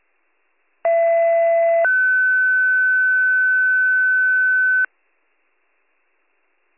Two-tone page, heard as a one-second tone, then a three-second tone, about 670 then 1500 Hz. It is heard from 0.8 to 4.9 s.